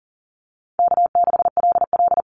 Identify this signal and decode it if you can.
Morse code, heard as a keyed tone, about 42 wpm, 700 Hz. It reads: K6LL
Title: Easy Call Signs - 40wpm